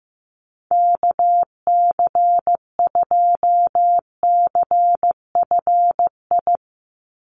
Morse code, heard as a keyed tone, about 15 wpm, 700 Hz. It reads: KC2CFI
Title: USA Call Signs 15wpm